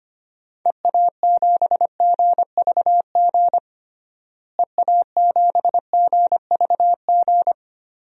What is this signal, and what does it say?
Morse code, heard as a keyed tone, about 25 wpm, 700 Hz. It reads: EA7GVG EA7GVG
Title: DXCC-100 Call Signs - 6 Characters - Repeated 3x - 25wpm